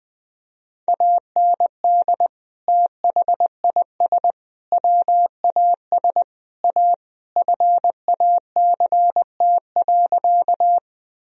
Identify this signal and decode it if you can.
Morse code, heard as a keyed tone, about 21 wpm, 700 Hz. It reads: AND THIS WAS A FACT.
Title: Sentences from Top 1000 Words - Review Part 3 - 20wpm